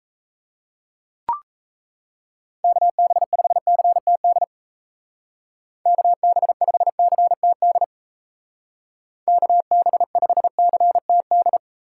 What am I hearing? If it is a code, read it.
Morse code, heard as a keyed tone, about 37 wpm, 700 Hz. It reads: KB5CTD KB5CTD KB5CTD